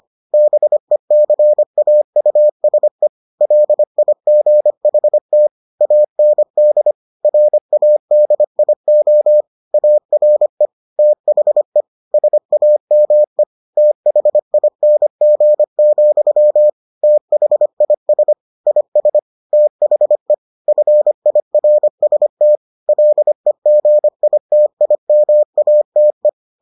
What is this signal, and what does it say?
Morse code, heard as a keyed tone, about 25 wpm, 600 Hz. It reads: BECAUSE LIGHT AND RADIO ARE THE SAME THING, THIS IS THE FIRST LEGITIMATE